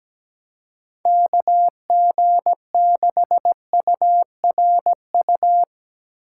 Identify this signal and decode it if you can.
Morse code, heard as a keyed tone, about 17 wpm, 700 Hz. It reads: KG6URU
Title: USA Call Signs - 4x Character Spacing - 17wpm